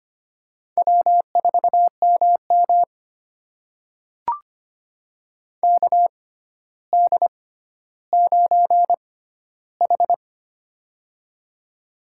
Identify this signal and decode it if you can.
Morse code, heard as a keyed tone, about 25 wpm, spaced as for 13 wpm, 700 Hz. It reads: W4MM KD9H
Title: USA Call Signs - 4 Characters - 6x Character Spacing - 25wpm